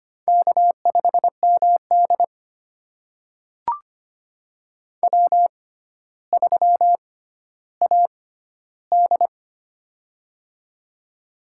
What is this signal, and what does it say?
Morse code, heard as a keyed tone, about 25 wpm, spaced as for 13 wpm, 700 Hz. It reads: K5MD W3AD
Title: USA Call Signs - 4 Characters - 6x Character Spacing - 25wpm